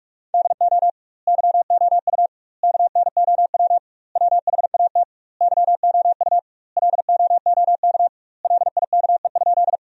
Morse code, heard as a keyed tone, about 41 wpm, 700 Hz. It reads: DO YOU KNOW WHAT YOU LOOK LIKE?
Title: 2x Word-Spacing - Sentences from Top 100 Words - 45wpm